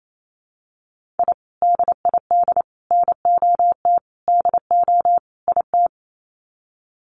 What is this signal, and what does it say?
Morse code, heard as a keyed tone, about 28 wpm, 700 Hz. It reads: I DID NOT DO IT
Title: Sets of 5 Words Encore2 28wpm